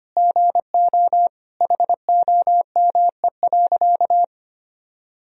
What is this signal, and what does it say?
Morse code, heard as a keyed tone, about 25 wpm, 700 Hz. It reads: GO HOME.
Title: Sentences from Top 1000 Words - Review Part 2 - 25wpm